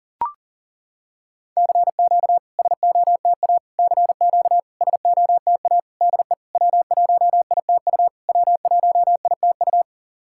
Morse code, heard as a keyed tone, about 37 wpm, 690 Hz. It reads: CQ SOTA CQ SOTA DE W1ITU W1ITU
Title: SOTA QSO Practice 40wpm